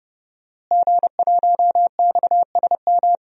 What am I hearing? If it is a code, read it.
Morse code, heard as a keyed tone, about 30 wpm, 700 Hz. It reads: G1XSM
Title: DXCC-100 Call Signs - 5 Characters - Repeated 2x - 30wpm